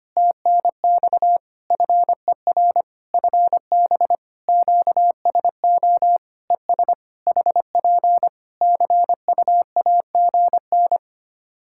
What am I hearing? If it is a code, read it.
Morse code, heard as a keyed tone, about 25 wpm, 700 Hz. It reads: TNX FER FB QSO ES HP CUAGN
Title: Practice QSO - Exchange 4 - Ending - 25wpm